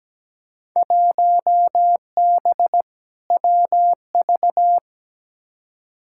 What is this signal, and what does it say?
Morse code, heard as a keyed tone, about 17 wpm, 700 Hz. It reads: 1B WV
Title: Field Day Exchanges - 17wpm